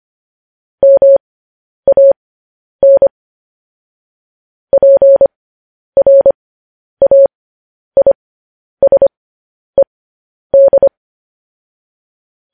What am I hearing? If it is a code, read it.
Morse code, heard as a keyed tone, about 25 wpm, spaced as for 10 wpm, 550 Hz. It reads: MAN PRAISED
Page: news-headlines-in-morse-code-at-10-wpm